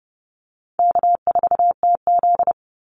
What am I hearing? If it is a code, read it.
Morse code, heard as a keyed tone, about 30 wpm, 700 Hz. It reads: K4TZ
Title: Easy Call Signs - 30wpm